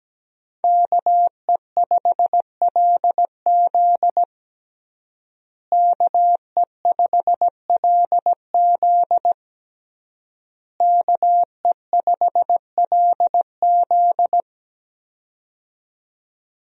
Morse code, heard as a keyed tone, about 17 wpm, 700 Hz. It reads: KE5LZ KE5LZ KE5LZ
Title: USA Call Signs - 5 Characters - Repeated 3x - 17wpm